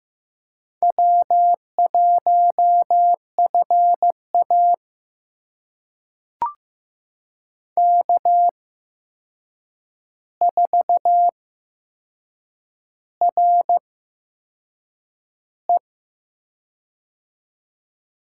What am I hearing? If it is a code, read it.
Morse code, heard as a keyed tone, about 15 wpm, spaced as for 6 wpm, 700 Hz. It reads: W1FA K4RE
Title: USA Call Signs - 4 Characters - 8x Character Spacing - 15wpm